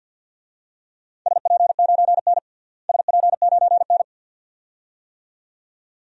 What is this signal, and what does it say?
Morse code, heard as a keyed tone, about 52 wpm, 700 Hz. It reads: SP9N SP9N
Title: DXCC-100 Call Signs - 4 Characters - Repeated 2x - 50wpm